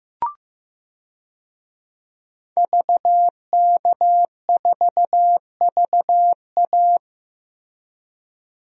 Morse code, heard as a keyed tone, about 16 wpm, 700 Hz. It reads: VK4VA
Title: DXCC-100 Call Signs - 5 Characters - Repeated 3x - 15wpm